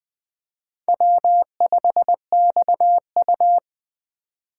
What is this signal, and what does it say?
Morse code, heard as a keyed tone, about 21 wpm, 700 Hz. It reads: W5XU
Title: K1USN SST QSO Practice - 20wpm